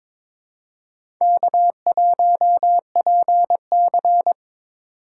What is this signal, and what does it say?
Morse code, heard as a keyed tone, about 23 wpm, 700 Hz. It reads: K1PC
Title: USA Call Signs - 4x Character Spacing - 22wpm